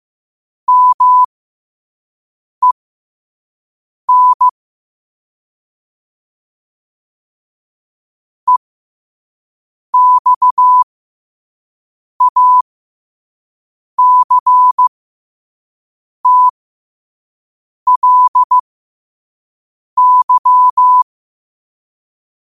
Morse code, heard as a keyed tone, about 15 wpm, spaced as for 5 wpm, 1000 Hz. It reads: MEN EXACTLY
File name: qotdfiles / 5wpm